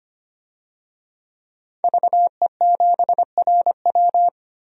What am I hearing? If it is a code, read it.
Morse code, heard as a keyed tone, about 25 wpm, 700 Hz. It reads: VE7RW
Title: DXCC-100 Call Signs - 5 Characters - Repeated 3x - 25wpm